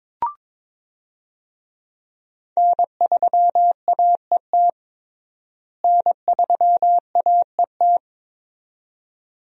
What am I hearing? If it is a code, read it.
Morse code, heard as a keyed tone, about 22 wpm, 700 Hz. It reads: N3AET N3AET
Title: USA Call Signs - 5 Characters - Repeated 2x - 22wpm